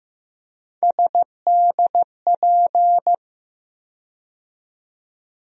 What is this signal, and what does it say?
Morse code, heard as a keyed tone, about 15 wpm, 700 Hz. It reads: SDP
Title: USA Call Sign Postfixes - 3 Letters 15wpm